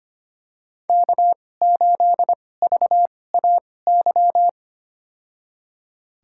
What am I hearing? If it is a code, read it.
Morse code, heard as a keyed tone, about 25 wpm, spaced as for 18 wpm, 700 Hz. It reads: K8VAY
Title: USA Call Signs - 5 Characters - 2x Character Spacing - 25wpm